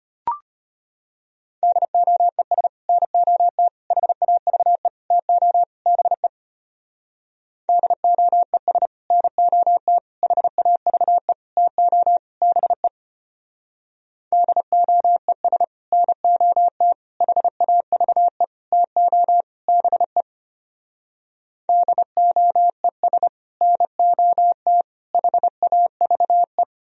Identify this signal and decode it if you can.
Morse code, heard as a keyed tone, about 32 wpm, 700 Hz. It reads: DOES NOT HAVE TO BE DOES NOT HAVE TO BE DOES NOT HAVE TO BE DOES NOT HAVE